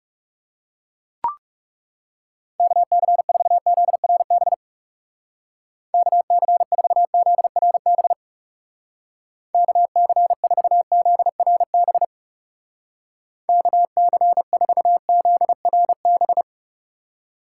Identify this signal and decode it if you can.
Morse code, heard as a keyed tone, about 37 wpm, 700 Hz. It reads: KC4ZRB KC4ZRB KC4ZRB KC4ZRB
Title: Speed Racing - USA 2x3 Call Signs 30wpm